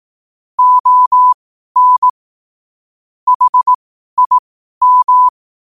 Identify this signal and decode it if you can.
Morse code, heard as a keyed tone, about 18 wpm, spaced as for 12 wpm, 1000 Hz. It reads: ON HIM
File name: qotdfiles / 12wpm